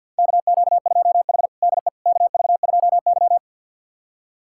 Morse code, heard as a keyed tone, about 52 wpm, 700 Hz. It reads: KQ2H DE KV2Y
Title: USA Callsign DE Callsign - 4 Characters - 50wpm